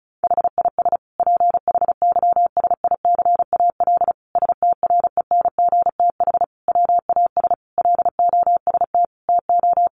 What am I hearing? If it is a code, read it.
Morse code, heard as a keyed tone, about 34 wpm, 700 Hz. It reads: HIS PHYSICAL STRENGTH WAS LOST TO
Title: Sentences from Top 1400 Words 35wpm